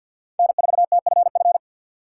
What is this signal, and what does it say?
Morse code, heard as a keyed tone, about 52 wpm, 700 Hz. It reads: N4TFF
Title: USA Call Signs - 5 Characters - 3x Character Spacing - 50wpm